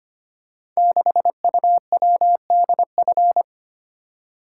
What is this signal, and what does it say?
Morse code, heard as a keyed tone, about 25 wpm, 700 Hz. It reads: 6UWDF
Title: Code Groups - 5 Letter:Numbers - 25wpm